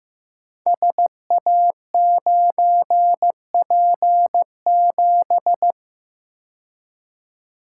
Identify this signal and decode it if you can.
Morse code, heard as a keyed tone, about 15 wpm, 700 Hz. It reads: SA9P7
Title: Code Groups - 5 Letter:Numbers - 15wpm